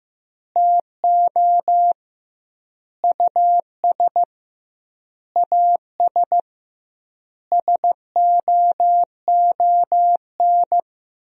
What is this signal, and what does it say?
Morse code, heard as a keyed tone, about 15 wpm, 700 Hz. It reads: TO US AS SOON
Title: 2x Word-Spacing - Sentences from Top 500 Words - 15wpm